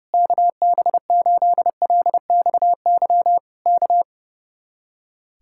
Morse code, heard as a keyed tone, about 28 wpm, 700 Hz. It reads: KB8LXY K
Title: Speed-Racing Common QSO Phrases - 30wpm